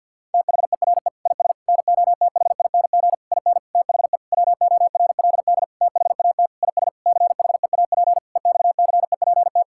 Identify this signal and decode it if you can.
Morse code, heard as a keyed tone, about 51 wpm, 700 Hz. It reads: THERE IS NOTHING IN THE WORLD THAT IS CHEAP EXCEPT
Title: Sentences from Top 1600 Words 50wpm